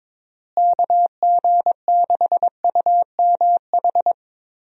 Morse code, heard as a keyed tone, about 22 wpm, 700 Hz. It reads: KG6UMH
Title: USA Call Signs - 6 Characters - Repeated 2x - 22wpm